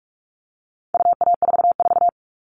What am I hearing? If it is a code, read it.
Morse code, heard as a keyed tone, about 44 wpm, 700 Hz. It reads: UA44
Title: All Cut Numbers - 4-Digit 45wpm